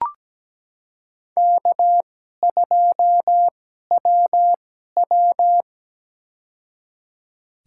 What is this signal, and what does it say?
Morse code, heard as a keyed tone, about 17 wpm, spaced as for 12 wpm, 700 Hz. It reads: K2WW
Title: USA Call Signs - 4 Characters - 2x Character Spacing - 17wpm